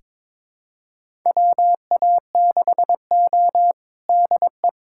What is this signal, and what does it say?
Morse code, heard as a keyed tone, about 22 wpm, 700 Hz. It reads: WA6O DE
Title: USA Callsign DE Callsign - 4 Characters - 22wpm